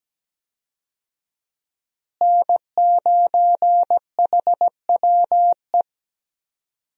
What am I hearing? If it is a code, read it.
Morse code, heard as a keyed tone, about 17 wpm, 700 Hz. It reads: N9HWE
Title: USA Call Signs - 5 Characters - Repeated 2x - 17wpm